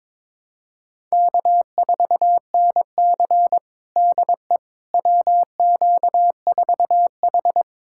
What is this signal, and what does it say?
Morse code, heard as a keyed tone, about 22 wpm, 700 Hz. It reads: K4NC DE WQ4H
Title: USA Callsign DE Callsign - 4 Characters - 22wpm